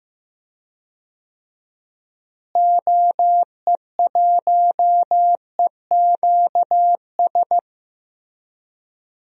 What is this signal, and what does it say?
Morse code, heard as a keyed tone, about 15 wpm, 700 Hz. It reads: OE1EQS
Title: DXCC-100 Call Signs - 6 Characters - Repeated 3x - 15wpm